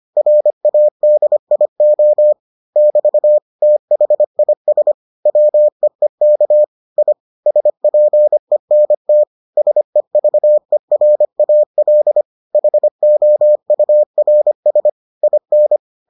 Morse code, heard as a keyed tone, about 25 wpm, 600 Hz. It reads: RADIO <BT> THIS WEEK I SPENT SEVERAL HOURS IN